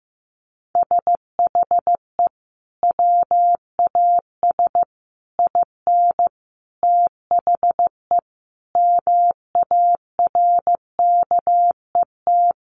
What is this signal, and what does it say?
Morse code, heard as a keyed tone, about 14 wpm, 700 Hz. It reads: SHE WAS IN THE MARKET
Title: Sentences from Top 600 Words 15wpm